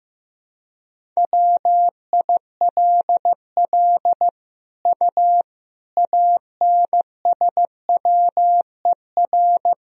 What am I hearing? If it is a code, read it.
Morse code, heard as a keyed tone, about 15 wpm, 700 Hz. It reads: WILL U ANSWER
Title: Sentences from Top 400 Words and Abbreviations 15wpm